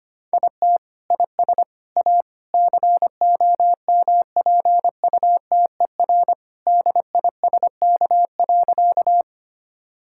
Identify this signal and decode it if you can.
Morse code, heard as a keyed tone, about 25 wpm, 700 Hz. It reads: IT IS A COMPUTER DISK.
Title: Sentences from Top 1000 Words - Review Part 6 - 25wpm